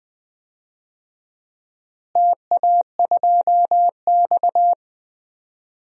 Morse code, heard as a keyed tone, about 21 wpm, 700 Hz. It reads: TA2X
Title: DXCC-100 Call Signs - 4 Characters - Repeated 3x - 20wpm